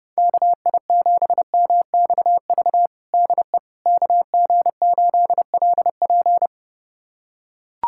Morse code, heard as a keyed tone, about 30 wpm, 700 Hz. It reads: KI7MXV DE KG8LP
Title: USA Callsign DE Callsign 30wpm